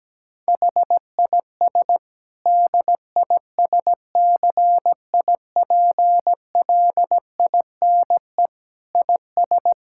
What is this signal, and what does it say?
Morse code, heard as a keyed tone, about 17 wpm, 700 Hz. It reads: HIS DISCIPLINE IS
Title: Sentences from Top 1000 Words - Review Part 6 - 17wpm